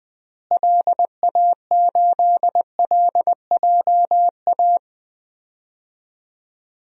Morse code, heard as a keyed tone, about 20 wpm, 700 Hz. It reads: LA8LJA
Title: DXCC-100 Call Signs - 6 Characters - Repeated 2x - 20wpm